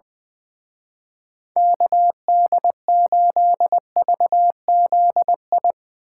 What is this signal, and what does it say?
Morse code, heard as a keyed tone, about 22 wpm, 700 Hz. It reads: KD8VZI
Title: Speed Racing - USA 2x3 Call Signs 20wpm